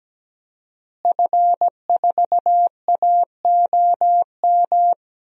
Morse code, heard as a keyed tone, about 17 wpm, 700 Hz. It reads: F4AOM
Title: DXCC-100 Call Signs - 5 Characters - Repeated 2x - 17wpm